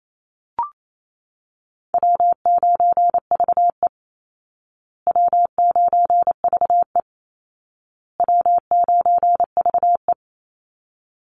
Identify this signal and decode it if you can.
Morse code, heard as a keyed tone, about 28 wpm, 700 Hz. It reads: W9VE W9VE W9VE
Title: Easy Call Signs Repeated 3x - 28wpm